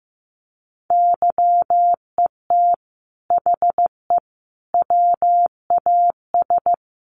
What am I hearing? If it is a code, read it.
Morse code, heard as a keyed tone, about 15 wpm, 700 Hz. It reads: YET HE WAS
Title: Sentences from Top 1100 Words 15wpm